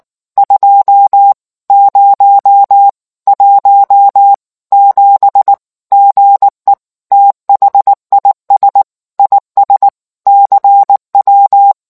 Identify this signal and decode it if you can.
Morse code, heard as a keyed tone, about 19 wpm, 800 Hz. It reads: 2017 GE THIS IS CW